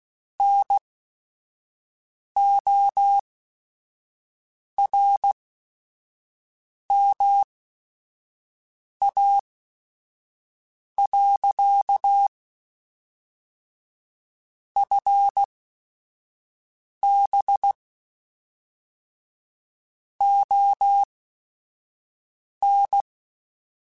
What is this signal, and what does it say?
Morse code, heard as a keyed tone, about 16 wpm, spaced as for 5 wpm, 800 Hz. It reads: NORMA. FB ON
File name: QSOs / QSOs-mp3s / WD0FBT 1@5